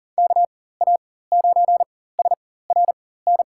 Morse code, heard as a keyed tone, about 40 wpm, spaced as for 19 wpm, 700 Hz. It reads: KA9SRN
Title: USA Call Signs - 6 Characters - 4x Character Spacing - 40wpm